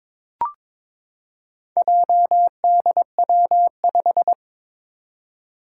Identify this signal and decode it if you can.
Morse code, heard as a keyed tone, about 22 wpm, 700 Hz. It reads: JDW5
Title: Four Challenging Letter-Number Combinations 22wpm